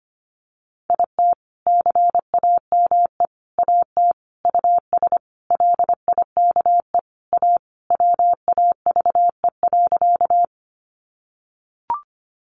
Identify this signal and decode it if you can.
Morse code, heard as a keyed tone, about 25 wpm, 700 Hz. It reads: IT CAME AT US LIKE A WAVE.
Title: Sentences from Top 1400 Words 25wpm